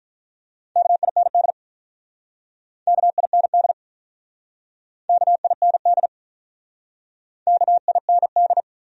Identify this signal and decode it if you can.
Morse code, heard as a keyed tone, about 46 wpm, 700 Hz. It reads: KIND KIND KIND KIND